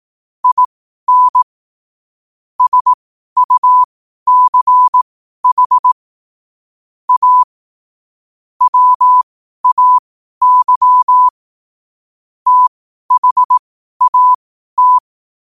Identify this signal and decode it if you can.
Morse code, heard as a keyed tone, about 18 wpm, spaced as for 12 wpm, 1000 Hz. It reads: IN SUCH A WAY THAT
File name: qotdfiles / 12wpm